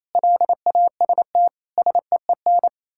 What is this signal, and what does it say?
Morse code, heard as a keyed tone, about 27 wpm, 700 Hz. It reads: LAST SEEN
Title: Pangram Challenge v2 - 28wpm